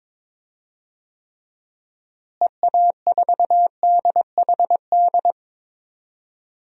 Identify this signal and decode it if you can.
Morse code, heard as a keyed tone, about 21 wpm, 700 Hz. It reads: EA4DHD
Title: DXCC-100 Call Signs - 6 Characters - Repeated 2x - 22wpm